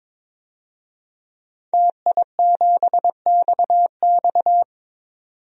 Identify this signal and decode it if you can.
Morse code, heard as a keyed tone, about 22 wpm, 700 Hz. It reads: TI7XX
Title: DXCC-100 Call Signs - 5 Characters - 22wpm